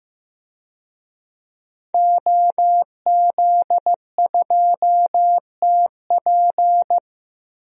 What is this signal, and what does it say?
Morse code, heard as a keyed tone, about 15 wpm, 700 Hz. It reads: OZ2TP
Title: DXCC-100 Call Signs - 5 Characters - 15wpm